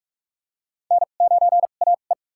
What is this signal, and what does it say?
Morse code, heard as a keyed tone, about 46 wpm, spaced as for 33 wpm, 700 Hz. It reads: N9AE
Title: USA Call Signs - 2x Character Spacing - 45wpm